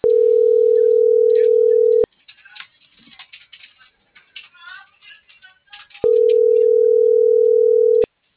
Ringback tone (North America), 440 + 480 Hz.